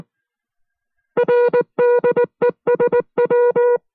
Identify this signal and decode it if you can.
Morse code, heard as a keyed tone, about 19 wpm, 480 Hz. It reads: RDESW